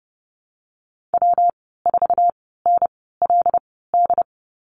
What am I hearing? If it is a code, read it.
Morse code, heard as a keyed tone, about 30 wpm, spaced as for 17 wpm, 700 Hz. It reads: W4NLD
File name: course / 30:10wpm / 116 Lesson 19g - Letter w - Call Signs 30wpm